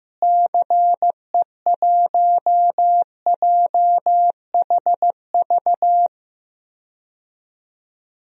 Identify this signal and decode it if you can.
Morse code, heard as a keyed tone, about 16 wpm, 700 Hz. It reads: CE1JHV
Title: DXCC-100 Call Signs - 6 Characters - Repeated 3x - 15wpm